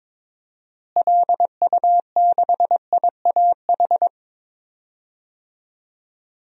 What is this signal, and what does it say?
Morse code, heard as a keyed tone, about 22 wpm, 700 Hz. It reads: LU6IAH
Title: DXCC-100 Call Signs - 6 Characters - Repeated 2x - 22wpm